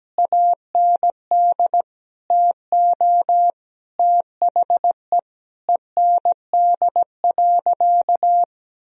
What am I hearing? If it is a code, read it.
Morse code, heard as a keyed tone, about 17 wpm, 700 Hz. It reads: AND TO THE END.
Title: Sentences from Top 1000 Words - Review Part 1 - 17wpm